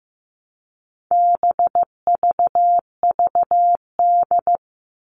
Morse code, heard as a keyed tone, about 15 wpm, 700 Hz. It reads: BVVD
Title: All Cut Numbers - 4-Digit 15wpm